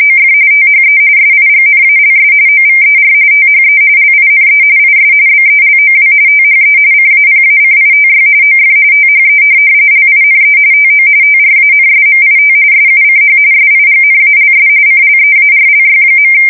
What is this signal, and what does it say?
RTTY, about 45.45 baud. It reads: OF THE SIGNALS OVER THE LINE IS BELOW THE CAPACITY OF THE LINE TO TRANSMIT SUCH SIGNALS CLEARLY.
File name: fsk170 / printing-telegraph